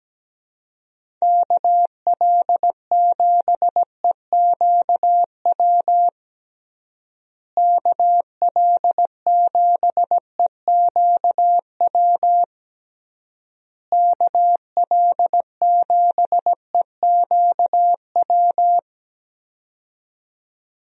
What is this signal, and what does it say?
Morse code, heard as a keyed tone, about 17 wpm, 700 Hz. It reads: KL7EQW KL7EQW KL7EQW
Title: DXCC-100 Call Signs - 6 Characters - Repeated 3x - 17wpm